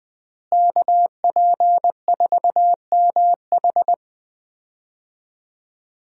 Morse code, heard as a keyed tone, about 20 wpm, 700 Hz. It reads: KP4MH
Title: DXCC-100 Call Signs - 5 Characters - Repeated 2x - 20wpm